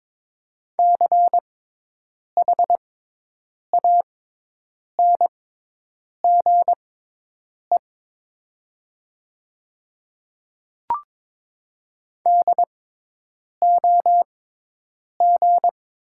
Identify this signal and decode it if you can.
Morse code, heard as a keyed tone, about 22 wpm, spaced as for 8 wpm, 700 Hz. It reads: CHANGE DOG
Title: 6x Character Spacing - Top 500 Words - 22wpm